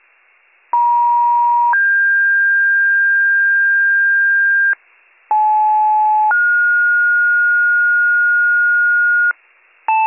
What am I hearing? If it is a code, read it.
Two-tone page, heard as a one-second tone, then a three-second tone, about 950 then 1600 Hz, about 860 then 1430 Hz, with noise. It is heard from 0.7 to 9.3 s.